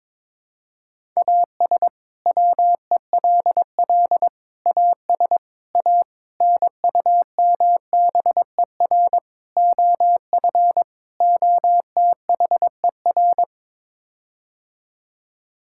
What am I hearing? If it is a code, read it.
Morse code, heard as a keyed tone, about 22 wpm, 700 Hz. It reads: AS WELL AS A NUMBER OF OTHER
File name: practice / 22wpm / Sets of 7 Words - 22wpm